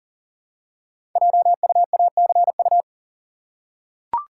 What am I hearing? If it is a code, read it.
Morse code, heard as a keyed tone, about 40 wpm, 690 Hz. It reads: JUACU
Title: Code Groups - 5 Letters - 40wpm